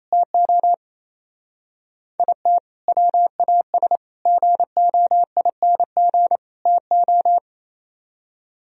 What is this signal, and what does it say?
Morse code, heard as a keyed tone, about 37 wpm, 700 Hz. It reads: TO IT WAS GOING TO